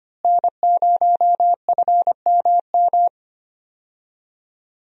Morse code, heard as a keyed tone, about 25 wpm, 700 Hz. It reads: N0FMM
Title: USA Call Signs - 5 Characters - Repeated 3x - 25wpm